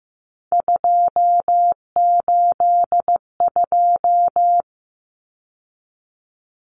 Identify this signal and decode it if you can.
Morse code, heard as a keyed tone, about 15 wpm, 700 Hz. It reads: 282
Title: Numbers - 3-Digit 15wpm